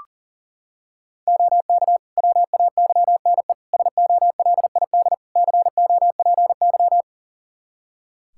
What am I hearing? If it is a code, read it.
Morse code, heard as a keyed tone, about 40 wpm, 700 Hz. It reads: OK WAYNE SOLID COPY
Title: Practice QSO - Exchange 3 - More Info - 40wpm